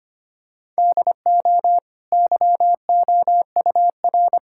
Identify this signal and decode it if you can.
Morse code, heard as a keyed tone, about 25 wpm, 700 Hz. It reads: DO YOUR
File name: Easy Sentences with Hints - 25wpm